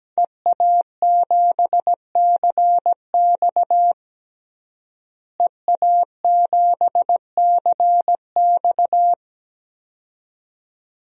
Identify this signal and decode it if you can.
Morse code, heard as a keyed tone, about 17 wpm, 700 Hz. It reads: EA7CX EA7CX
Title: DXCC-100 Call Signs - 5 Characters - Repeated 2x - 17wpm